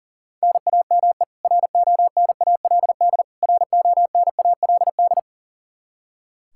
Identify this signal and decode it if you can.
Morse code, heard as a keyed tone, about 40 wpm, 700 Hz. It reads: NAME RONALD RONALD
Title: Practice QSO - Exchange 1 - Beginning - 40wpm